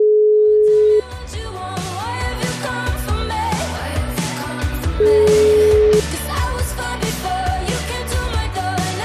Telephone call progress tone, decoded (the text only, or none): ringback tone